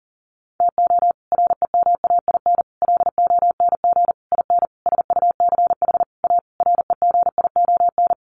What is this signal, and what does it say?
Morse code, heard as a keyed tone, about 40 wpm, 700 Hz. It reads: TO REMAIN LONG IN SUCH A REGION
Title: Sentences from Top 1500 Words 40wpm